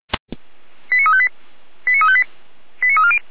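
Five-tone selective call (ZVEI), heard as quick runs of five tones, tones of about 70 ms. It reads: 89358 89358 89340
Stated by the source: Five-tone ZVEI selective call received on an SDR.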